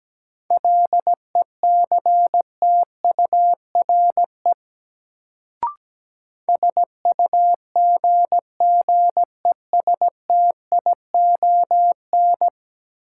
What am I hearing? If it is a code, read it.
Morse code, heard as a keyed tone, about 17 wpm, 700 Hz. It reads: LECTURE SUGGESTION
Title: Sentences from Top 1600 Words - New Word Drill - 17wpm